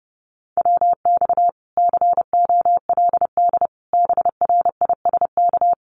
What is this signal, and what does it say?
Morse code, heard as a keyed tone, about 30 wpm, 700 Hz. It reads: WX COLD BRISK
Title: Common QSO Phrases Encore 30wpm